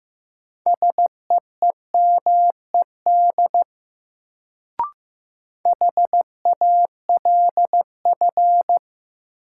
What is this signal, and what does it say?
Morse code, heard as a keyed tone, about 15 wpm, 700 Hz. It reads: SEEMED HALF
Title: Sets of 3 Words - New Word Drill - 15wpm